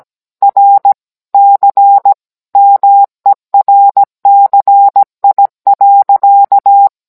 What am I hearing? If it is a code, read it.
Morse code, heard as a keyed tone, about 17 wpm, 800 Hz. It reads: RC MERCI.